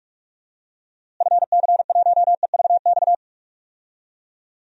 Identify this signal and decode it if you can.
Morse code, heard as a keyed tone, about 48 wpm, 700 Hz. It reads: FC1EVX
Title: DXCC-100 Call Signs - 6 Characters - Repeated 3x - 45wpm